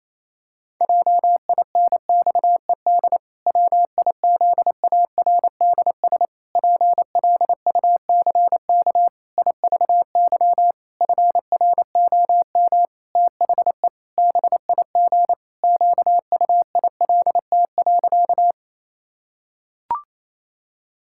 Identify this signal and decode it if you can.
Morse code, heard as a keyed tone, about 28 wpm, 700 Hz. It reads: JINXED WIZARDS PLUCK IVY FROM THE BIG QUILT.
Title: Pangram Challenge v2 - 28wpm